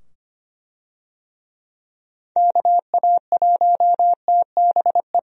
Morse code, heard as a keyed tone, about 25 wpm, 700 Hz. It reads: KA1TBE
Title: USA Call Signs - 6 Characters - 25wpm